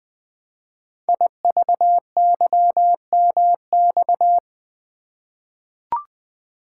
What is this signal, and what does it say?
Morse code, heard as a keyed tone, about 20 wpm, 700 Hz. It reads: IVYMX
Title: Code Groups - 5 Letters - 20wpm